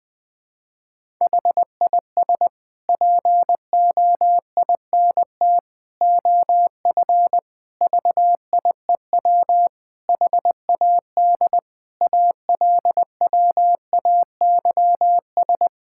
Morse code, heard as a keyed tone, about 20 wpm, 700 Hz. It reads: HIS POINT OF VIEW HAD ALWAYS
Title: Sentences from Top 1000 Words - Review Part 6 - 20wpm